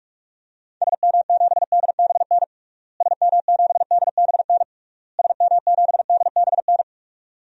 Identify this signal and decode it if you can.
Morse code, heard as a keyed tone, about 41 wpm, 700 Hz. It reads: SM7DBN SM7DBN SM7DBN
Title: DXCC-100 Call Signs - 6 Characters - Repeated 3x - 45wpm